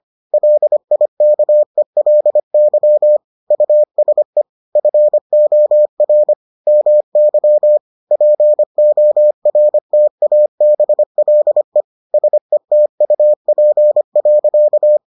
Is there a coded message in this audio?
Yes: Morse code from 0.3 to 15.0 s, heard as a keyed tone, about 25 wpm, 600 Hz. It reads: LIKELY USE FOR MY PORTABLE SETUP.